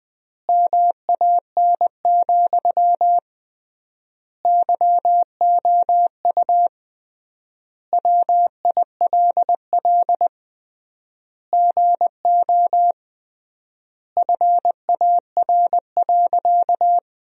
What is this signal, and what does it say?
Morse code, heard as a keyed tone, about 20 wpm, 700 Hz. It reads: MAN, YOU WILL GO FAR.
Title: 3x Word-Spacing - Sentences from Top 400 Words - 20wpm